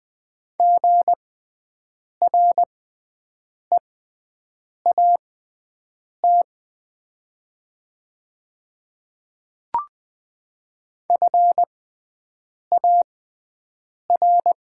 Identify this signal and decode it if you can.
Morse code, heard as a keyed tone, about 20 wpm, spaced as for 7 wpm, 700 Hz. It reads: GREAT FAR
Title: 6x Character Spacing - Top 500 Words - 20wpm